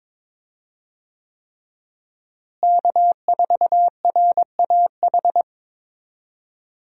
Morse code, heard as a keyed tone, about 22 wpm, 700 Hz. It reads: K4RAH
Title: USA Call Signs - 5 Characters - Repeated 3x - 22wpm